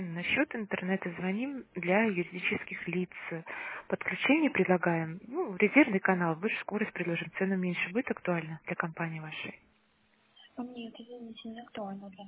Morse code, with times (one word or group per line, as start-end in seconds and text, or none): none